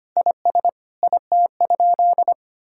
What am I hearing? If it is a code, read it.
Morse code, heard as a keyed tone, about 25 wpm, 700 Hz. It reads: IS IT?
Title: Sentences from Top 1000 Words - Review Part 5 - 25wpm